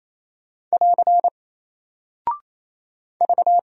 Morse code, heard as a keyed tone, about 28 wpm, 700 Hz. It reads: <AR> V
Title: Single Character (Letter:Number:Punctuation:Prosign) Alt 28wpm